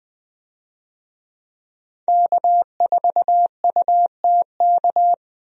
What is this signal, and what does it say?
Morse code, heard as a keyed tone, about 20 wpm, 700 Hz. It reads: K4UTK
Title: USA Call Signs - 5 Characters - 20wpm